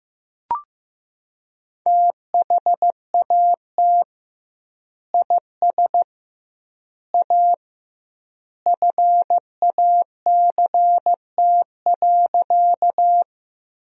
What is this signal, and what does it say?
Morse code, heard as a keyed tone, about 15 wpm, 700 Hz. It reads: THAT IS A FACT.
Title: 2x Word-Spacing - Sentences from Top 500 Words - 15wpm